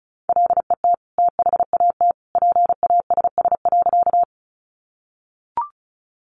Morse code, heard as a keyed tone, about 34 wpm, 700 Hz. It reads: LET THAT PASS.
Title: Sentences from Top 400 Words 35wpm